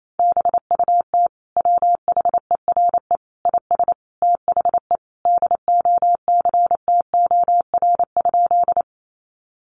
Morse code, heard as a keyed tone, about 28 wpm, 700 Hz. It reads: BUT WHERE IS THE DOCTOR?
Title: Sentences from Top 1500 Words 28wpm